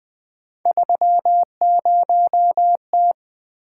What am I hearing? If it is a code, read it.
Morse code, heard as a keyed tone, about 20 wpm, 700 Hz. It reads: 30T
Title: Three Letter-Number Combinations - 20wpm